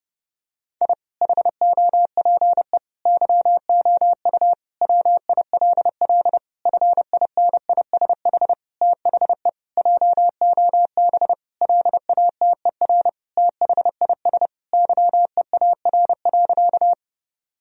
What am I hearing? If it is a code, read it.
Morse code, heard as a keyed tone, about 32 wpm, 700 Hz. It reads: I HOPE YOU WILL FINISH THE JOB LATER THIS YEAR.
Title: Sentences from Top 1000 Words - Review Part 3 - 30wpm